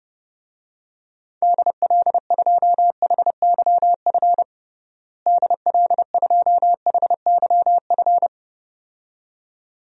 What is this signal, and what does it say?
Morse code, heard as a keyed tone, about 30 wpm, 700 Hz. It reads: DL2HYF DL2HYF
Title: DXCC-100 Call Signs - 6 Characters - Repeated 2x - 30wpm